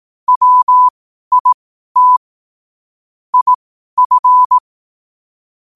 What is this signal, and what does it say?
Morse code, heard as a keyed tone, about 18 wpm, spaced as for 12 wpm, 1000 Hz. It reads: WIT IF
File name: qotdfiles / 12wpm